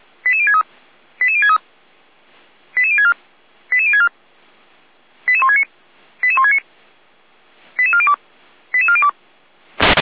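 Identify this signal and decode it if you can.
Five-tone selective call (ZVEI), heard as quick runs of five tones, tones of about 70 ms. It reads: 88063 88063 88064 88064 88169 88169 88402 88402
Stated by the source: ZVEI five-tone selective call received with an SDR.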